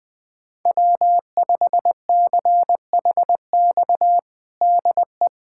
Morse code, heard as a keyed tone, about 20 wpm, 700 Hz. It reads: W5CHX DE
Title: USA Callsign DE Callsign 20wpm